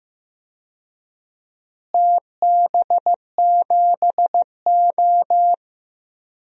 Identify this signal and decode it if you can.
Morse code, heard as a keyed tone, about 15 wpm, 700 Hz. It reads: TB7O
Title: DXCC-100 Call Signs - 4 Characters - Repeated 2x - 15wpm